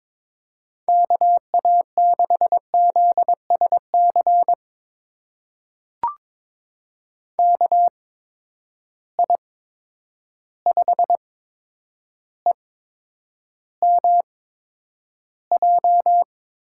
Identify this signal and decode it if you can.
Morse code, heard as a keyed tone, about 22 wpm, spaced as for 9 wpm, 700 Hz. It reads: KA6ZSC KI5EMJ
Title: USA Call Signs - 6 Characters - 8x Character Spacing - 22wpm